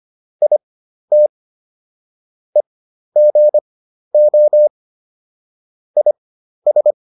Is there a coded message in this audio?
Yes: Morse code from 0.4 to 6.9 s, heard as a keyed tone, about 25 wpm, spaced as for 12 wpm, 600 Hz. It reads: IT EGO IS